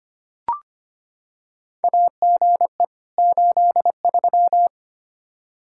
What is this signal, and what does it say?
Morse code, heard as a keyed tone, about 25 wpm, 700 Hz. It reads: AGE 83
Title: Practice QSO - Exchange 3 - More Info - 25wpm